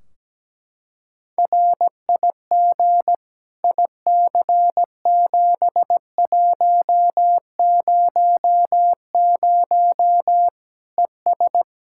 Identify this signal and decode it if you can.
Morse code, heard as a keyed tone, about 17 wpm, 700 Hz. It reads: RIG IC7100 ES
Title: Practice QSO - Exchange 2 - Info - 17wpm